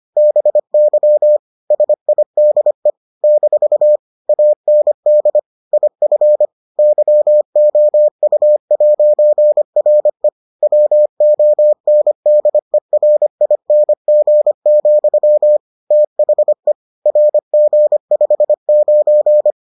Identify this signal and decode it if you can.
Morse code, heard as a keyed tone, about 25 wpm, 600 Hz. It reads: BY SIDE - AND IF YOU'RE WONDERING, THE RG59